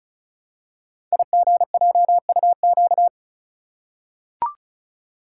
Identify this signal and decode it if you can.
Morse code, heard as a keyed tone, about 36 wpm, 690 Hz. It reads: IGJUQ
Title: Code Groups - 5 Letters - 35wpm